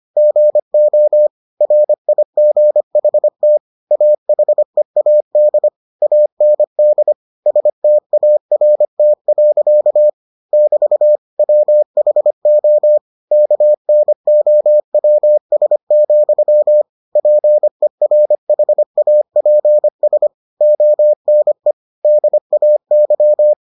Morse code, heard as a keyed tone, about 25 wpm, 600 Hz. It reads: GO RIGHT AHEAD AND START. <BT> WHO KNOWS, PERHAPS ONE DAY